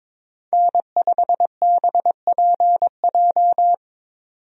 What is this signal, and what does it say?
Morse code, heard as a keyed tone, about 22 wpm, 700 Hz. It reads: N5BPJ
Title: USA Call Signs - 5 Characters - 22wpm